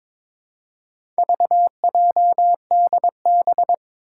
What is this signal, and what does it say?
Morse code, heard as a keyed tone, about 21 wpm, 700 Hz. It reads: VJDB
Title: Four Challenging Letter-Number Combinations 22wpm